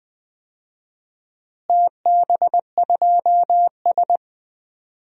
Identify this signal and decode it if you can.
Morse code, heard as a keyed tone, about 21 wpm, 700 Hz. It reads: TB2S
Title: DXCC-100 Call Signs - 4 Characters - Repeated 3x - 20wpm